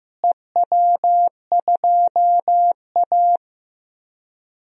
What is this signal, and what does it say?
Morse code, heard as a keyed tone, about 15 wpm, 700 Hz. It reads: EW2A
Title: DXCC-100 Call Signs - 4 Characters - Repeated 2x - 15wpm